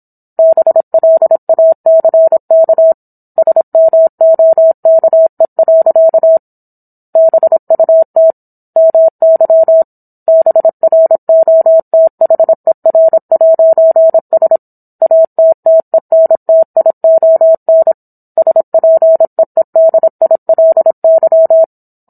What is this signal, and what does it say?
Morse code, heard as a keyed tone, about 27 wpm, 650 Hz. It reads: BLACK SMOKE. BUT MY BROTHER'S ATTENTION SPEEDILY